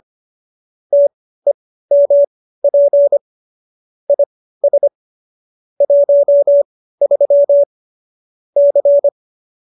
Morse code, heard as a keyed tone, about 25 wpm, spaced as for 15 wpm, 570 Hz. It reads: TEMP IS 13 C